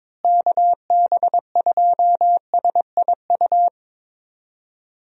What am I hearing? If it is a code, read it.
Morse code, heard as a keyed tone, about 22 wpm, 700 Hz. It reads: KB2SIU
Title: USA Call Signs - 6 Characters - Repeated 2x - 22wpm